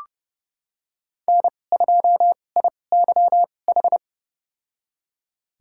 Morse code, heard as a keyed tone, about 32 wpm, spaced as for 22 wpm, 700 Hz. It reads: N2IYH
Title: USA Call Signs - 5 Characters - 2x Character Spacing - 30wpm